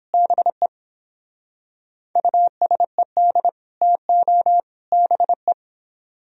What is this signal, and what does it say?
Morse code, heard as a keyed tone, about 29 wpm, 700 Hz. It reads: BE USED TO BE